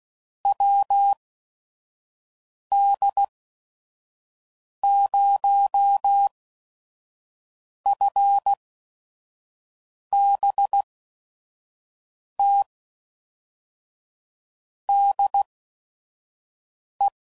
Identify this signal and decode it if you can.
Morse code, heard as a keyed tone, about 16 wpm, spaced as for 5 wpm, 800 Hz. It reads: WD0FBT DE
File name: CW Code Groups @5WPM / QSOs / WD0FBT 2